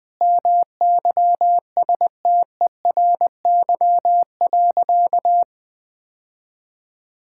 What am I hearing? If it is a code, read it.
Morse code, heard as a keyed tone, about 20 wpm, 700 Hz. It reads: MYSTERY.
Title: Sentences with 6-Word Phrases - 20wpm